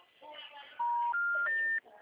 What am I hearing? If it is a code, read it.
Special information tone, with speech.
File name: out- -2018- - - .396